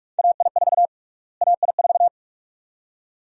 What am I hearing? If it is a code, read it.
Morse code, heard as a keyed tone, about 46 wpm, 690 Hz. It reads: AI4 AI4
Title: USA Call Sign Prefixes - Repeated 2x - 45wpm